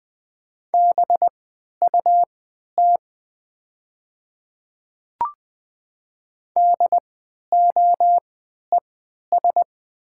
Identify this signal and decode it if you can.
Morse code, heard as a keyed tone, about 20 wpm, spaced as for 11 wpm, 700 Hz. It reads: BUT DOES
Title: 3x Character Spacing - Top 500 Words - 20wpm